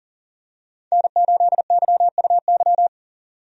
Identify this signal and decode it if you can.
Morse code, heard as a keyed tone, about 40 wpm, 700 Hz. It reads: N8YUY
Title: USA Call Signs - 5 Characters - 40wpm